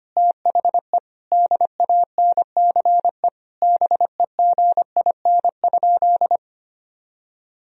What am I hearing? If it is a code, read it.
Morse code, heard as a keyed tone, about 25 wpm, 700 Hz. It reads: THE DANCE BEGIN?
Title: Sentences from Top 1000 Words - Review Part 4 - 25wpm